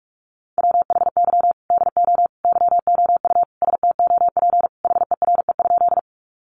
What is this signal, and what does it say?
Morse code, heard as a keyed tone, about 44 wpm, 700 Hz. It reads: WHY DO YOU STOP HERE?
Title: Sentences from Top 300 Words 45wpm